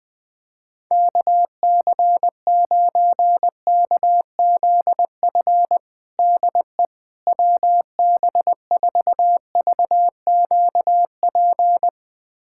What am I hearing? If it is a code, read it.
Morse code, heard as a keyed tone, about 20 wpm, 700 Hz. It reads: KC9KZF DE WB4VQP
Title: USA Callsign DE Callsign - 6 Characters - 20wpm